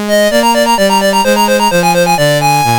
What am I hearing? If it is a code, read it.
There is no code.